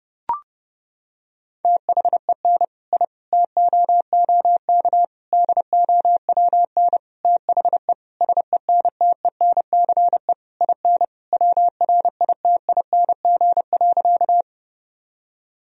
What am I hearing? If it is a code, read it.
Morse code, heard as a keyed tone, about 30 wpm, 700 Hz. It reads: THEN I TOOK DOWN THE SENTENCE IN WRITING.
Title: Sentences from Top 1000 Words - Review Part 1 - 30wpm